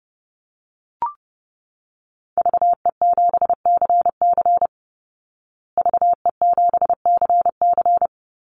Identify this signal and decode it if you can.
Morse code, heard as a keyed tone, about 30 wpm, 700 Hz. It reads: VE7CC VE7CC
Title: Call Signs Repeated 2x - 30wpm